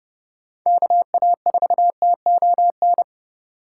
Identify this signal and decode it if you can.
Morse code, heard as a keyed tone, about 30 wpm, 700 Hz. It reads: KA4TON
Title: USA Call Signs - 6 Characters - 4x Character Spacing - 30wpm